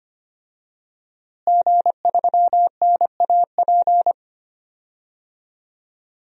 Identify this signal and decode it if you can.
Morse code, heard as a keyed tone, about 25 wpm, 700 Hz. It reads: G3NAP
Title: DXCC-100 Call Signs - 5 Characters - Repeated 3x - 25wpm